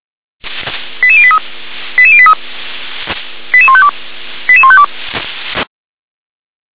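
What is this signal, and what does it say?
Five-tone selective call (ZVEI), heard as quick runs of five tones, tones of about 70 ms. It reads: 88083 88083 88152 88152
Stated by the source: ZVEI five-tone selective call received by SDR.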